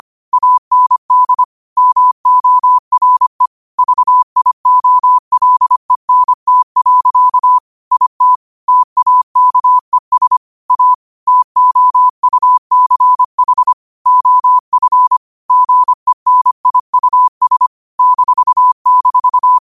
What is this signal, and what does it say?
Morse code, heard as a keyed tone, about 25 wpm, 1000 Hz. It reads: AND MORE VIOLENT. IT TAKES A TOUCH OF GENIUS --